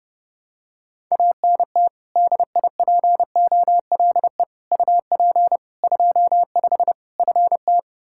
Morse code, heard as a keyed tone, about 30 wpm, 700 Hz. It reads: ANT DIPOLE UP 25 FT
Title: Practice QSO - Exchange 2 - Info - 30wpm